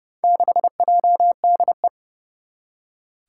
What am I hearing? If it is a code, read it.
Morse code, heard as a keyed tone, about 30 wpm, 700 Hz. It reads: 6JDE
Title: Four Challenging Letter-Number Combinations 30wpm